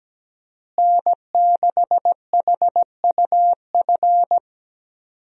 Morse code, heard as a keyed tone, about 17 wpm, 700 Hz. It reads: N6HUF
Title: USA Call Signs - 5 Characters - Repeated 2x - 17wpm